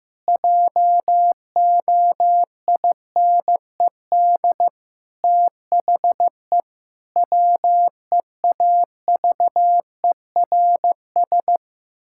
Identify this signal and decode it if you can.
Morse code, heard as a keyed tone, about 16 wpm, 700 Hz. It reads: JOINED THE WEAVERS
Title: Pangram Challenge v2 - 15wpm